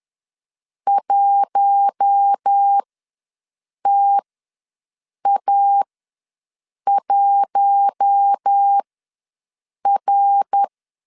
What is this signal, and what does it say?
Morse code, heard as a keyed tone, about 10 wpm, spaced as for 6 wpm, 800 Hz. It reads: 1TA1R
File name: LEZIONI / 5 LBVC / 5 LB ws 10wpm 3616